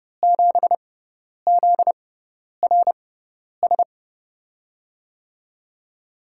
Morse code, heard as a keyed tone, about 31 wpm, spaced as for 10 wpm, 700 Hz. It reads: 7ZRS
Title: USA Call Signs - 6 Characters - 6x Character Spacing - 30wpm